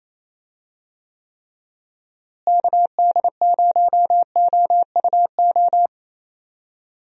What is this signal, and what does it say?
Morse code, heard as a keyed tone, about 27 wpm, 700 Hz. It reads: KD0OUO
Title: USA Call Signs - 6 Characters - Repeated 3x - 28wpm